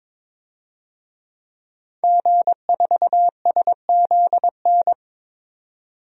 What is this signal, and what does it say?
Morse code, heard as a keyed tone, about 22 wpm, 700 Hz. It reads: G4SZN
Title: DXCC-100 Call Signs - 5 Characters - 22wpm